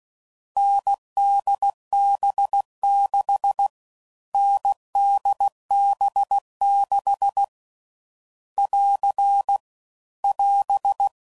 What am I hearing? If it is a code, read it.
Morse code, heard as a keyed tone, about 16 wpm, 800 Hz. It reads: NDB6 NDB6 <AR> <AS>
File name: Combinations / Difficult Combos / NDB NDB6